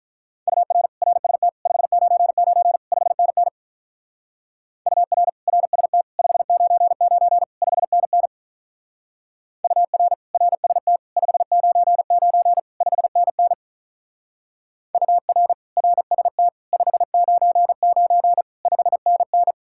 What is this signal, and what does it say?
Morse code, heard as a keyed tone, about 46 wpm, 700 Hz. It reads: UR RST 599 5NN UR RST 599 5NN UR RST 599 5NN UR RST 599 5NN